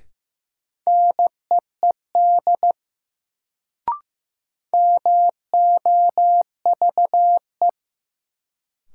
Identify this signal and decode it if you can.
Morse code, heard as a keyed tone, about 15 wpm, 700 Hz. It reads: NEED MOVE
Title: Sentences from Top 200 Words - New Word Drill - 15wpm